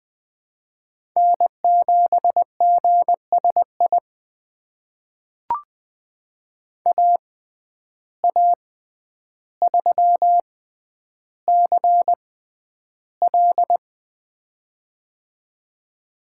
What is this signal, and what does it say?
Morse code, heard as a keyed tone, about 20 wpm, spaced as for 10 wpm, 700 Hz. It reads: N7GSI AA3CL
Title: USA Call Signs - 5 Characters - 6x Character Spacing - 20wpm